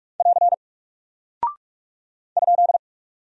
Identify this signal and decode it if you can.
Morse code, heard as a keyed tone, about 44 wpm, 700 Hz. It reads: <AR> ?
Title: Single Character (Letter:Number:Punctuation:Prosign) Alt 45wpm